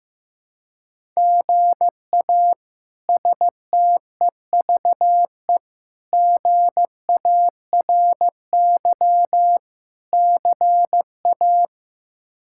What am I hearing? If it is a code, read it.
Morse code, heard as a keyed tone, about 15 wpm, 700 Hz. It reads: GA STEVE GARY CA
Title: K1USN SST QSO Practice - 15wpm